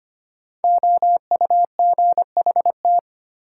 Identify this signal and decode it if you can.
Morse code, heard as a keyed tone, about 25 wpm, 700 Hz. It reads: OUGHT
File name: 5 Letter Words - First 3 Letters Given - 25wpm